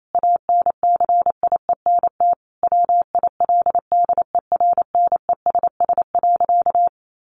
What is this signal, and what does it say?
Morse code, heard as a keyed tone, about 28 wpm, 700 Hz. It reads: ANCIENT WILDERNESS.
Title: A Sound Of Thunder 28wpm